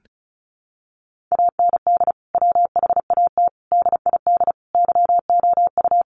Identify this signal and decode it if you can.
Morse code, heard as a keyed tone, about 34 wpm, 700 Hz. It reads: AND WHAT DID YOU
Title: Sentences from Top 100 Words 35wpm